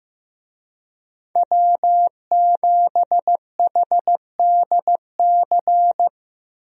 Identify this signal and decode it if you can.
Morse code, heard as a keyed tone, about 15 wpm, 700 Hz. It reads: W7HDC
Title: USA Call Signs - 5 Characters - Repeated 2x - 15wpm